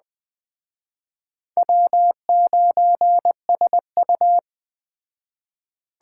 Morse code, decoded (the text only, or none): W9SU